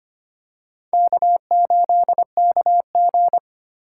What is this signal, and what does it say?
Morse code, heard as a keyed tone, about 25 wpm, 700 Hz. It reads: K8KG
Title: USA Call Signs - 4 Characters - 4x Character Spacing - 25wpm